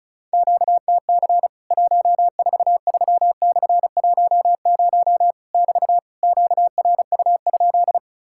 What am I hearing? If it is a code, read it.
Morse code, heard as a keyed tone, about 34 wpm, 700 Hz. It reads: QTC 143/10 <BT> QRU?
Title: Worked All Europe DX Contest - QTC Exchange - 35wpm